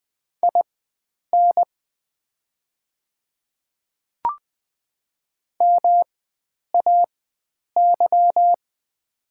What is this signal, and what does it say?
Morse code, heard as a keyed tone, about 20 wpm, spaced as for 9 wpm, 700 Hz. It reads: IN MAY
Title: 4x Character Spacing - Top 100 Words - 20wpm